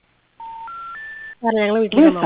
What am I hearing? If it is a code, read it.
Special information tone, with speech.